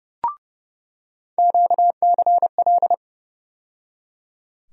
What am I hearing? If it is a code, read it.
Morse code, heard as a keyed tone, about 30 wpm, 700 Hz. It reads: QCL
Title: USA Call Sign Postfixes - 3 Letters 30wpm